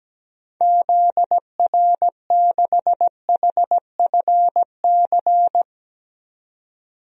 Morse code, heard as a keyed tone, about 17 wpm, 700 Hz. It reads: ZR6HFC
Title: DXCC-100 Call Signs - 6 Characters - 17wpm